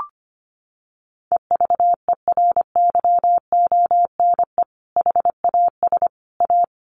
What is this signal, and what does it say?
Morse code, heard as a keyed tone, about 25 wpm, 700 Hz. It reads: EVERYONE HAS A
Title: Sentences from Top 1200 Words 25wpm